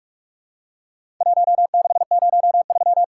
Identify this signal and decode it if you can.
Morse code, heard as a keyed tone, about 46 wpm, 700 Hz. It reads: 1603
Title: Worked All Europe DX Contest - QTC Exchange - 45wpm